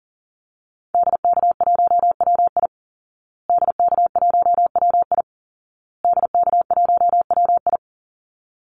Morse code, heard as a keyed tone, about 40 wpm, 700 Hz. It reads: DK1WI DK1WI DK1WI
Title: Call Signs Repeated 2x - 40wpm